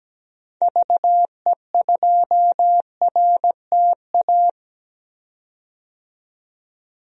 Morse code, heard as a keyed tone, about 17 wpm, 700 Hz. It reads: VE2RTA
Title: DXCC-100 Call Signs - 6 Characters - 17wpm